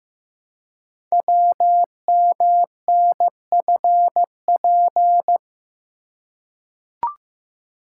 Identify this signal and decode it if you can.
Morse code, heard as a keyed tone, about 15 wpm, 700 Hz. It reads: WMNFP
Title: Code Groups - 5 Letters - 15wpm